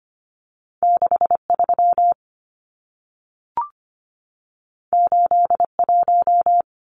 Morse code, heard as a keyed tone, about 25 wpm, 700 Hz. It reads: 63 81
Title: Numbers - 2-Digit 25wpm